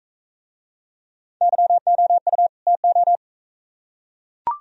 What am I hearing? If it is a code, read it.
Morse code, heard as a keyed tone, about 44 wpm, 700 Hz. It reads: YOU TO